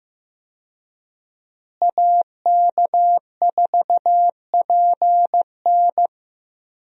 Morse code, heard as a keyed tone, about 15 wpm, 700 Hz. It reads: AK4PN
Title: USA Call Signs - 5 Characters - Repeated 2x - 15wpm